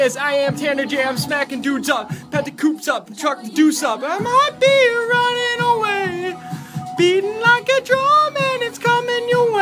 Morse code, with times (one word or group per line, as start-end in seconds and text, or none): none